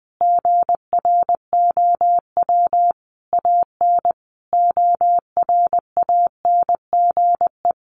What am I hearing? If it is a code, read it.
Morse code, heard as a keyed tone, about 20 wpm, 700 Hz. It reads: GROW AN ORANGE
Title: Easy Phrases 20wpm